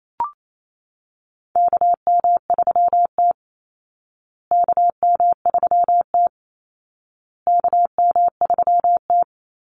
Morse code, heard as a keyed tone, about 28 wpm, 700 Hz. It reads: KM3T KM3T KM3T
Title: Call Signs Repeated 2x - 28wpm